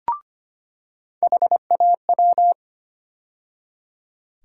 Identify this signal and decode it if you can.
Morse code, heard as a keyed tone, about 25 wpm, 700 Hz. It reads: HAW
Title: USA Call Sign Postfixes - 3 Letters 25wpm